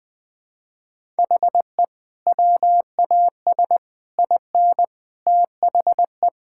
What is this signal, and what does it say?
Morse code, heard as a keyed tone, about 20 wpm, 700 Hz. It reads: HE WAS IN THE
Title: Sentences with 7-Word Phrases - 20wpm